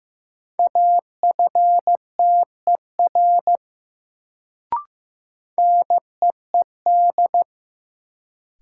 Morse code, heard as a keyed tone, about 15 wpm, 700 Hz. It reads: AFTER NEED
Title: Sentences from Top 200 Words - New Word Drill - 15wpm